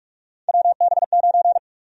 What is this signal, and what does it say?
Morse code, heard as a keyed tone, about 46 wpm, 700 Hz. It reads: WB9
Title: USA Call Sign Prefixes - 45wpm